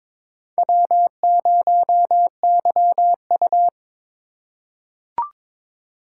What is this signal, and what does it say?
Morse code, heard as a keyed tone, about 22 wpm, 700 Hz. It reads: W0YU
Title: USA Call Signs - 4 Characters - 8x Character Spacing - 22wpm